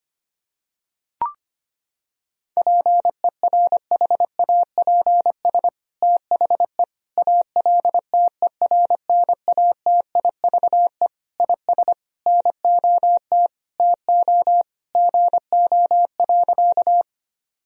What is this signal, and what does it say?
Morse code, heard as a keyed tone, about 25 wpm, 700 Hz. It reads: PERHAPS THE ALTERNATIVE IS NOT TO GO.
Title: Sentences from Top 1000 Words - Review Part 4 - 25wpm